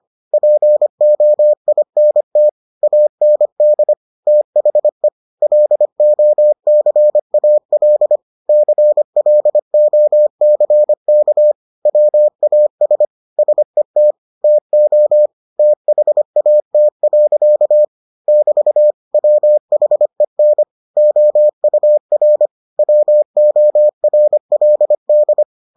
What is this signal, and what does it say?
Morse code, heard as a keyed tone, about 25 wpm, 600 Hz. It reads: POINT AND THE LOCAL CLOCK WAS SET TO THAT. <BT> WHEN OUR WORLD